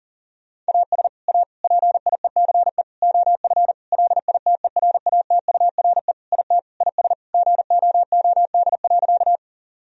Morse code, heard as a keyed tone, about 37 wpm, 700 Hz. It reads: AS A PIECE OF LITERATURE IT IS GOOD.
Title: Sentences from Top 1000 Words - Review Part 3 - 40wpm